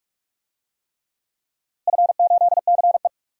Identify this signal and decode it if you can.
Morse code, heard as a keyed tone, about 41 wpm, 700 Hz. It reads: F8CE
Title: DXCC-100 Call Signs - 4 Characters - Repeated 3x - 45wpm